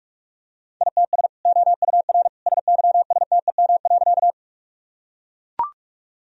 Morse code, heard as a keyed tone, about 42 wpm, 700 Hz. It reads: ITS OUR SYSTEM.
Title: Sentences from Top 1000 Words - Review Part 3 - 45wpm